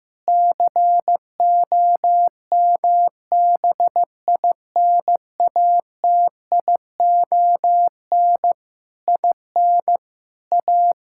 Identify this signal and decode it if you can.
Morse code, heard as a keyed tone, about 16 wpm, 700 Hz. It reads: COMBINATION IN A
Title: Sentences from Top 1000 Words - Review Part 6 - 15wpm